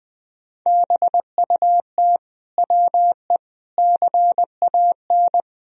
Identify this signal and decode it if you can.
Morse code, heard as a keyed tone, about 20 wpm, 700 Hz. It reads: BUT WE CAN
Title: Sentences from Top 100 Words and Abbreviations 20wpm